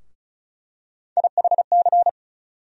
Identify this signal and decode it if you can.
Morse code, heard as a keyed tone, about 34 wpm, 700 Hz. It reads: IHC
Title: USA Call Sign Postfixes - 3 Letters 35wpm